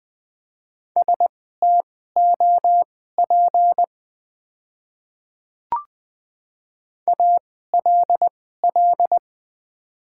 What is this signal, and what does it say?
Morse code, heard as a keyed tone, about 20 wpm, spaced as for 14 wpm, 700 Hz. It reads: STOP ALL
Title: 2x Character Spacing - Top 300 Words - 20wpm